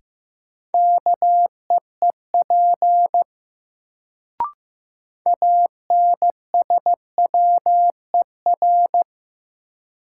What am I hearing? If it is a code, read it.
Morse code, heard as a keyed tone, about 15 wpm, 700 Hz. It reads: KEEP ANSWER
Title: Sentences from Top 300 Words - New Word Drill - 15wpm